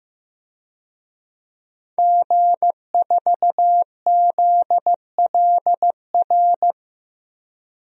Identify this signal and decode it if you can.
Morse code, heard as a keyed tone, about 15 wpm, 700 Hz. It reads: G4ZLR
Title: DXCC-100 Call Signs - 5 Characters - 15wpm